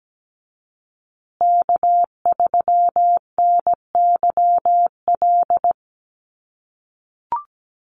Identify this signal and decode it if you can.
Morse code, heard as a keyed tone, about 17 wpm, 700 Hz. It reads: K3NYL
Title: USA Call Signs Repeated 2x 17wpm